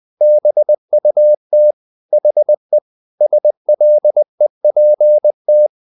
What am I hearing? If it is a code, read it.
Morse code, heard as a keyed tone, about 20 wpm, 600 Hz. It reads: BUT HE SLEPT
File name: the old man and the sea part 2 20wpm